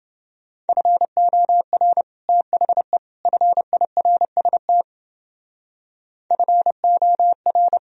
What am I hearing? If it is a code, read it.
Morse code, heard as a keyed tone, about 29 wpm, 700 Hz. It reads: FOR THE FIRST FOR